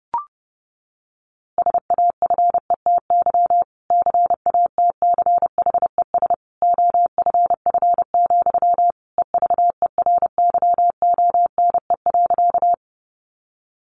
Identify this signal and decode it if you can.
Morse code, heard as a keyed tone, about 30 wpm, 700 Hz. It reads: SAFETY CATCHES OFF, EVERYONE.
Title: A Sound Of Thunder 30wpm